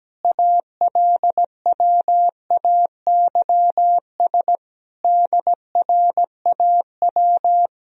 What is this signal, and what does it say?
Morse code, heard as a keyed tone, about 17 wpm, 700 Hz. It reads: ALWAYS DRAW
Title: Sentences from Top 1000 Words - Review Part 1 - 17wpm